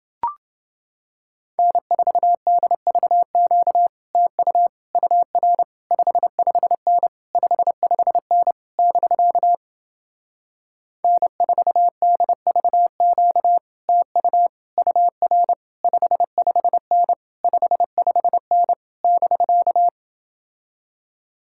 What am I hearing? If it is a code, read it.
Morse code, heard as a keyed tone, about 29 wpm, 700 Hz. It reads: N4DVQ TU UR 55N 55N <BK> N4DVQ TU UR 55N 55N <BK>